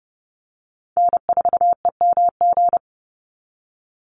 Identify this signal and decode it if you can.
Morse code, heard as a keyed tone, about 30 wpm, 700 Hz. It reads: N4EMG
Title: NA SOTA Chaser Callsigns 30wpm